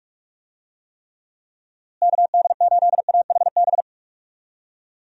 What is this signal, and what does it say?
Morse code, heard as a keyed tone, about 46 wpm, 700 Hz. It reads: KD8AHB
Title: USA Call Signs - 6 Characters - Repeated 2x - 45wpm